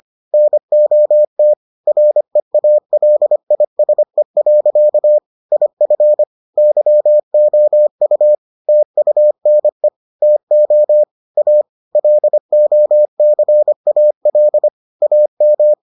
Morse code, heard as a keyed tone, about 25 wpm, 600 Hz. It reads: NOT REALISE. IF YOU TUNE TO A LOCAL AM